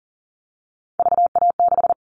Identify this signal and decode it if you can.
Morse code, heard as a keyed tone, about 40 wpm, 700 Hz. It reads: VA6
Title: Callsign Prefixes for Top 25 DXCC - 40wpm